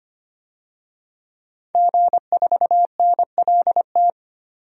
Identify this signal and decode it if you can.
Morse code, heard as a keyed tone, about 25 wpm, 700 Hz. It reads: G4NLT
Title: DXCC-100 Call Signs - 5 Characters - 25wpm